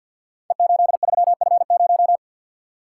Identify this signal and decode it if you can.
Morse code, heard as a keyed tone, about 52 wpm, 700 Hz. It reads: E83F0
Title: Code Groups - 5 Letter:Numbers - 50wpm